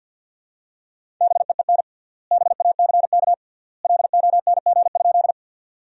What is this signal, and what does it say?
Morse code, heard as a keyed tone, about 52 wpm, 700 Hz. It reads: BEEN BACK LONG?
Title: 3x Word-Spacing - Sentences from Top 200 Words - 50wpm